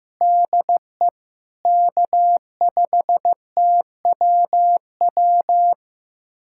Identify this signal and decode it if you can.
Morse code, heard as a keyed tone, about 16 wpm, 700 Hz. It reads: DE K5TWW
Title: USA Callsign DE Callsign - 5 Characters - 15wpm